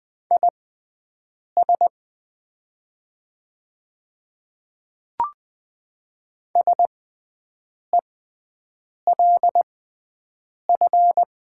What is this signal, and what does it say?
Morse code, heard as a keyed tone, about 20 wpm, spaced as for 7 wpm, 700 Hz. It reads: IS SELF
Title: 6x Character Spacing - Top 300 Words - 20wpm